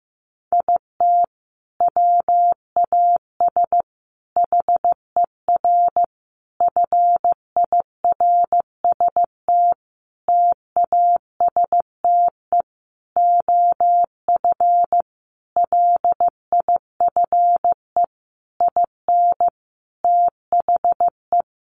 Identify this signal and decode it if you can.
Morse code, heard as a keyed tone, about 15 wpm, 700 Hz. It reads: IT WAS HER FIRST TASTE OF LIFE IN THE
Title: Sentences from Top 1100 Words 15wpm